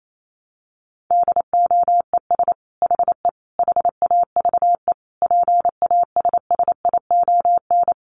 Morse code, heard as a keyed tone, about 28 wpm, 700 Hz. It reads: DOES HE HAVE PASSION
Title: Sentences from Top 1300 Words 28wpm